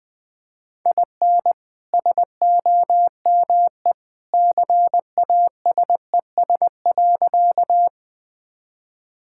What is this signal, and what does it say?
Morse code, heard as a keyed tone, about 20 wpm, 700 Hz. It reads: IN SOME CASES.
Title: Sentences with 2-Word Phrases - 20wpm